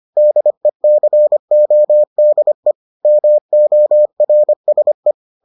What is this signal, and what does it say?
Morse code, heard as a keyed tone, about 25 wpm, 600 Hz. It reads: DECODE MORSE